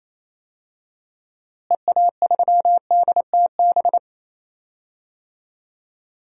Morse code, heard as a keyed tone, about 28 wpm, 700 Hz. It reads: EA3DTB
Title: DXCC-100 Call Signs - 6 Characters - Repeated 2x - 28wpm